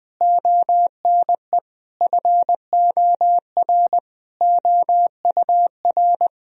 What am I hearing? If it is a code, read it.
Morse code, heard as a keyed tone, about 20 wpm, 700 Hz. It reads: ONE FOR OUR
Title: Sentences from Top 1000 Words - Review Part 2 - 20wpm